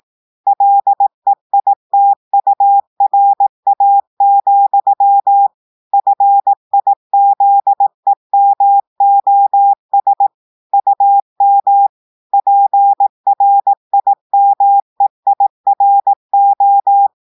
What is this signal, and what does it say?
Morse code, heard as a keyed tone, about 18 wpm, 800 Hz. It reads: LEITURA, FIZEMOS UM PRIMEIRO
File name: audios / portugais / portugais texte en clair 1 vitesse 18 mots minutes